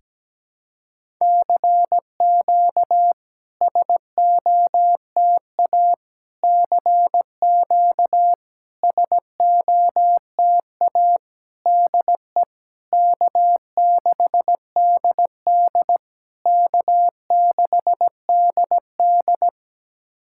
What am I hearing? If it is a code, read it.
Morse code, heard as a keyed tone, about 17 wpm, 700 Hz. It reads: CQ SOTA CQ SOTA DE K6DD K6DD
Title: SOTA QSO Practice 17wpm